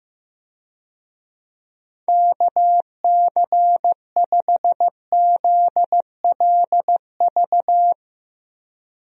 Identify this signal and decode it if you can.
Morse code, heard as a keyed tone, about 15 wpm, 700 Hz. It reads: KC5ZLV
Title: USA Call Signs - 6 Characters - 15wpm